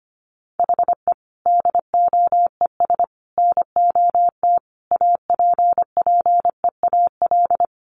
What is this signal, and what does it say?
Morse code, heard as a keyed tone, about 25 wpm, 700 Hz. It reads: HE DOES NOT APPEAL
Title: Sentences from Top 1300 Words 25wpm